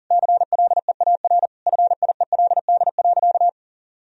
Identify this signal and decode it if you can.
Morse code, heard as a keyed tone, about 40 wpm, 700 Hz. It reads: CLEAR FIELD.
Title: Sentences from Top 1000 Words - Review Part 2 - 40wpm